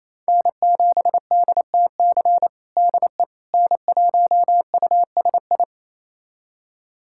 Morse code, heard as a keyed tone, about 28 wpm, 700 Hz. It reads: N7DTC DE N1USI
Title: USA Callsign DE Callsign - 5 Characters - 28wpm